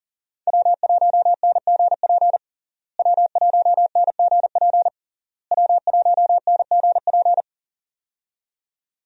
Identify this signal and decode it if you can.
Morse code, heard as a keyed tone, about 40 wpm, 700 Hz. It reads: W1NGP W1NGP W1NGP
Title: USA Call Signs - 5 Characters - Repeated 3x - 40wpm